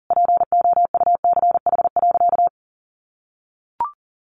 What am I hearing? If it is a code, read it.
Morse code, heard as a keyed tone, about 40 wpm, 700 Hz. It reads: POUCH.
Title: Easy Phrases 40wpm